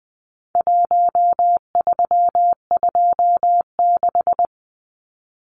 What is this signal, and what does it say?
Morse code, heard as a keyed tone, about 20 wpm, 700 Hz. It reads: 1326
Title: Numbers - Contest Style - 4-Digit 20wpm